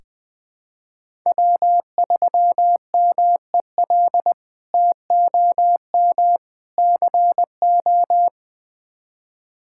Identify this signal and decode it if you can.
Morse code, heard as a keyed tone, about 20 wpm, 700 Hz. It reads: W3MEL TOM CO
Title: K1USN SST QSO Practice - 20wpm